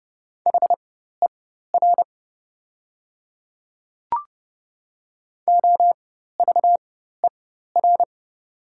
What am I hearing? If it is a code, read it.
Morse code, heard as a keyed tone, about 30 wpm, spaced as for 14 wpm, 700 Hz. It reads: HER OVER
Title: 4x Character Spacing - Top 100 Words - 30wpm